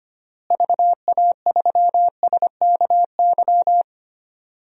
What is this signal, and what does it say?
Morse code, heard as a keyed tone, about 25 wpm, 700 Hz. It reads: VA3SKY
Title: DXCC-100 Call Signs - 6 Characters - Repeated 3x - 25wpm